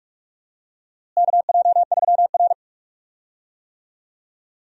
Morse code, heard as a keyed tone, about 46 wpm, 700 Hz. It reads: KJ3R
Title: USA Call Signs - 4 Characters - Repeated 3x - 45wpm